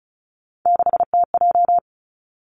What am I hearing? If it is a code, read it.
Morse code, heard as a keyed tone, about 36 wpm, 700 Hz. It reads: 6TJ
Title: Three Challenging Letter-Number Combinations 35wpm